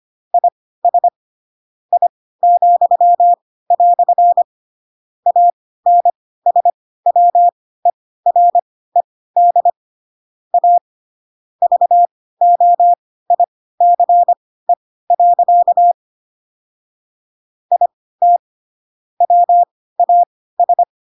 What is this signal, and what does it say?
Morse code, heard as a keyed tone, about 25 wpm, spaced as for 16 wpm, 700 Hz. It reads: IS I," ANSWERED A VOICE. IT WAS